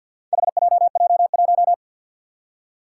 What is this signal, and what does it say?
Morse code, heard as a keyed tone, about 52 wpm, 700 Hz. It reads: HJJ1
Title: Four Challenging Letter-Number Combinations 50wpm